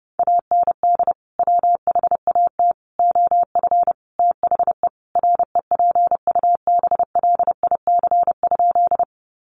Morse code, heard as a keyed tone, about 30 wpm, 700 Hz. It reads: AND WHAT OF THE REPUBLIC?
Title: Sentences from Top 1500 Words 30wpm